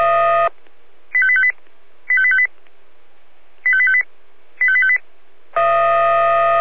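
Five-tone selective call (ZVEI), heard as quick runs of five tones, tones of about 70 ms. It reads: 85858 85858 85858 85858
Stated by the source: ZVEI five-tone selective call received with an SDR.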